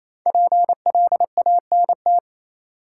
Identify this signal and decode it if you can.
Morse code, heard as a keyed tone, about 28 wpm, 700 Hz. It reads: PLANT
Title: Sentences from Top 1000 Words - Review Part 1 - New Word Drill - 28wpm